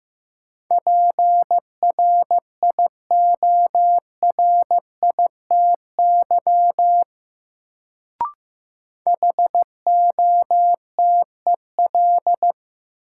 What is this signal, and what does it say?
Morse code, heard as a keyed tone, about 15 wpm, 700 Hz. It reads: PRIORITY HOTEL
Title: Sentences from Top 1600 Words - New Word Drill - 15wpm